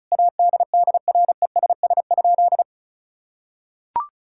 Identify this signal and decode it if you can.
Morse code, heard as a keyed tone, about 36 wpm, 700 Hz. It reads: ADDRESS?
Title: Sentences from Top 1000 Words - Review Part 6 - 35wpm